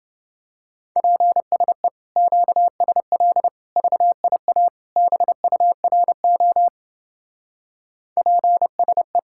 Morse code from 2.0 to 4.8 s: QSL VIA

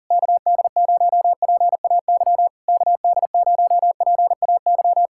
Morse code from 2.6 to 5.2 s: KD0PAY

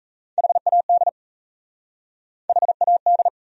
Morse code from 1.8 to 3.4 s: HAD

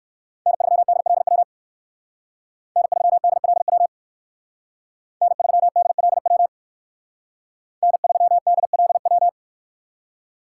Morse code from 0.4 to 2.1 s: N3DLW

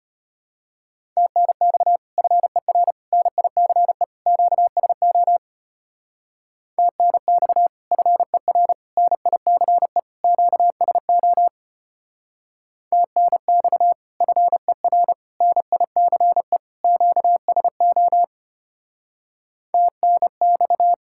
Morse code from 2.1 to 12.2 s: FER NICE QSO TNX FER NICE QSO

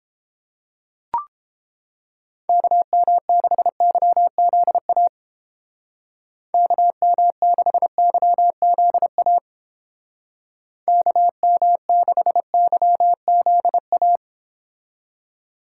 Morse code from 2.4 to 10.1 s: KM6YZA KM6YZA

